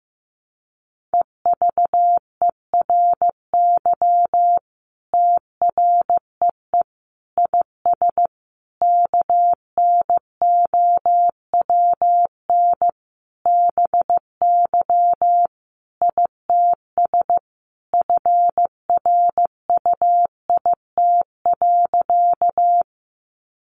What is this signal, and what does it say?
Morse code, heard as a keyed tone, about 15 wpm, 700 Hz. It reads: EVERY TREE IS KNOWN BY ITS FRUIT.
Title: Sentences from Top 1300 Words 15wpm